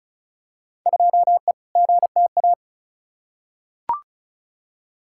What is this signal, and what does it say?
Morse code, heard as a keyed tone, about 36 wpm, 700 Hz. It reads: 2E GTA
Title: Field Day Exchanges Encore - 35wpm